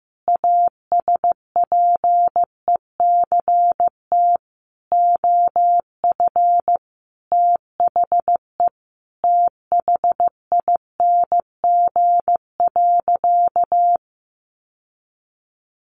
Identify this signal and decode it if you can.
Morse code, heard as a keyed tone, about 14 wpm, 700 Hz. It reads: ASPECT OF THE THING.
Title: Sentences from Top 1300 Words 15wpm